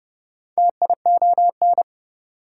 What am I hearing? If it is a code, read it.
Morse code, heard as a keyed tone, about 27 wpm, 700 Hz. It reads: TION